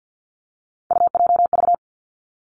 Morse code, heard as a keyed tone, about 50 wpm, 700 Hz. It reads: UJV
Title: Three Challenging Letter-Number Combinations 50wpm